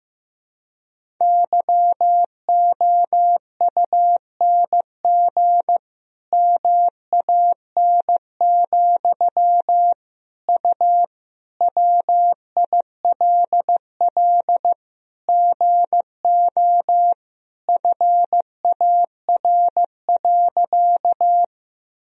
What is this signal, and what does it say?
Morse code, heard as a keyed tone, about 15 wpm, 700 Hz. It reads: YOUNG MAN, U WILL GO FAR.
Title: Sentences from Top 400 Words and Abbreviations 15wpm